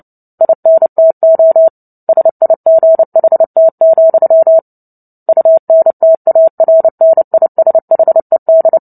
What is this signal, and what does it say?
Morse code, heard as a keyed tone, about 30 wpm, 650 Hz. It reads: INTO SIGHT, UNTARNISHED
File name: Book Two / 03-Chapter 3 - 29 WPM